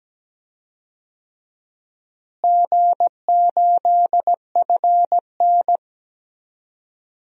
Morse code, heard as a keyed tone, about 17 wpm, 700 Hz. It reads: G8FN
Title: DXCC-100 Call Signs - 4 Characters - Repeated 3x - 17wpm